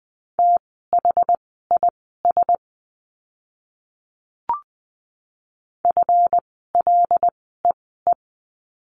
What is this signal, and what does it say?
Morse code, heard as a keyed tone, about 20 wpm, spaced as for 15 wpm, 700 Hz. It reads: THIS FLEE
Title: 139 Lesson 23f - Letter f - Words 20wpm